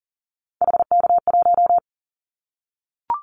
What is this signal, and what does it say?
Morse code, heard as a keyed tone, about 42 wpm, 700 Hz. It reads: HK1
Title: Callsign Prefixes for Top 100 DXCC - 40wpm